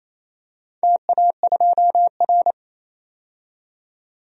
Morse code, heard as a keyed tone, about 28 wpm, 700 Hz. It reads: TA2R
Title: DXCC-100 Call Signs - 4 Characters - 28wpm